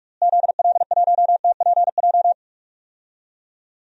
Morse code, heard as a keyed tone, about 46 wpm, 700 Hz. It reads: ZL1TPJ
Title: DXCC-100 Call Signs - 6 Characters - Repeated 3x - 45wpm